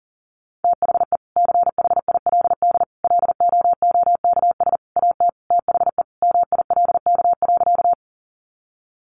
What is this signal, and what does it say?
Morse code, heard as a keyed tone, about 40 wpm, 700 Hz. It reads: THE CHILD LOOKS AT THE MILK.
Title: Additional Easy Phrases 40wpm